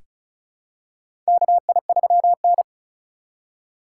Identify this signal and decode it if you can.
Morse code, heard as a keyed tone, about 36 wpm, 690 Hz. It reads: KI3N
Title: USA Call Signs - 3x Character Spacing - 35wpm